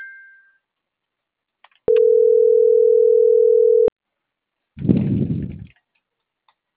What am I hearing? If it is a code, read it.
Ringback tone (North America), 440 + 480 Hz.